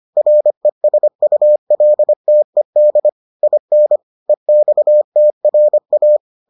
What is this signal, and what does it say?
Morse code, heard as a keyed tone, about 25 wpm, 600 Hz. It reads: RESULTED IN EXTRA